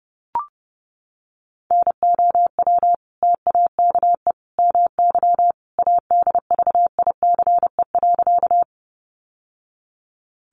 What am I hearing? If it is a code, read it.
Morse code, heard as a keyed tone, about 30 wpm, 700 Hz. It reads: NOW TAKE MY ADVICE.
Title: Sentences from Top 1100 Words 30wpm